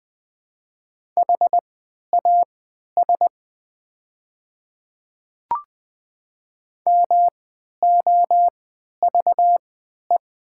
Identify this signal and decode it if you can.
Morse code, heard as a keyed tone, about 20 wpm, spaced as for 11 wpm, 700 Hz. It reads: HAS MOVE
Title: 3x Character Spacing - Top 200 Words - 20wpm